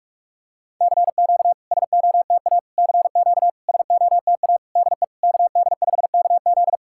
Morse code, heard as a keyed tone, about 44 wpm, 690 Hz. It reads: CQ SOTA CQ SOTA DE KD5KZ
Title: SOTA QSO Practice 45wpm